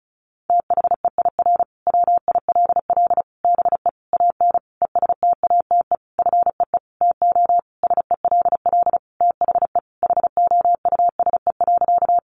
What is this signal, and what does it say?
Morse code, heard as a keyed tone, about 34 wpm, 700 Hz. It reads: THEIR WILL BE AN ESTATE FEE TO SELL THE HOUSE.
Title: Sentences from Top 1500 Words 35wpm